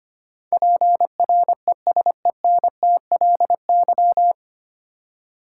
Morse code, heard as a keyed tone, about 25 wpm, 690 Hz. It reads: PRESENTLY
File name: practice / 25wpm / 9 Letter Words - First 3 Letters Given - 25wpm